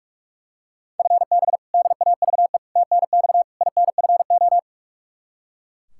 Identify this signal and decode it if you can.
Morse code, heard as a keyed tone, about 42 wpm, 700 Hz. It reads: FB DAVE TNX INFO
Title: K1USN SST QSO Practice 45wpm